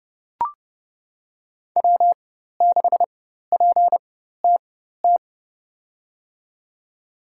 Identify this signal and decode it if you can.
Morse code, heard as a keyed tone, about 30 wpm, spaced as for 14 wpm, 700 Hz. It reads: W6PTT
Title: USA Call Signs - 5 Characters - 4x Character Spacing - 30wpm